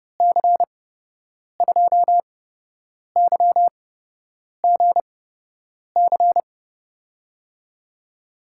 Morse code, heard as a keyed tone, about 30 wpm, spaced as for 8 wpm, 700 Hz. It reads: C2YGC
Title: USA Call Signs - 6 Characters - 8x Character Spacing - 30wpm